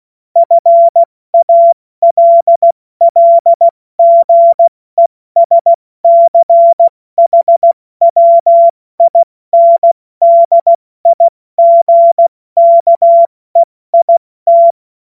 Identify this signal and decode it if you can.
Morse code, heard as a keyed tone, about 16 wpm, 680 Hz. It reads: FALLGESCHWINDIGKEIT
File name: Morsekurs / Witze / Witz 07. Dezember 2021, 70 F80